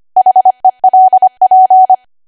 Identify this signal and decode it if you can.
Morse code, heard as a keyed tone, about 25 wpm, 750 Hz. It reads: HELP
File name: Archive / 25 WPM / 25WPM